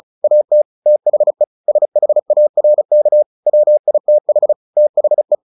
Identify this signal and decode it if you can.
Morse code, heard as a keyed tone, about 34 wpm, 600 Hz. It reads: AT THE SHARK WITH THE
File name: the old man and the sea 2 35wpm